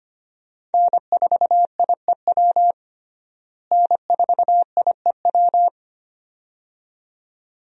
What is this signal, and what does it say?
Morse code, heard as a keyed tone, about 25 wpm, 700 Hz. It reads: N4IEW N4IEW
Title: USA Call Signs - 5 Characters - Repeated 2x - 25wpm